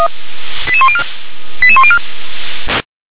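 Five-tone selective call (ZVEI), heard as quick runs of five tones, tones of about 70 ms. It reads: 88104 88104
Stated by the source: ZVEI five-tone selective call received with an SDR.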